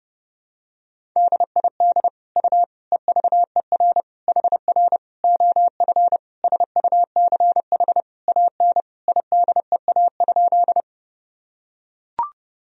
Morse code, heard as a keyed tone, about 30 wpm, 700 Hz. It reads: DID U EVER HR OF SUCH AN IDEA?
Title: Sentences from Top 400 Words and Abbreviations 30wpm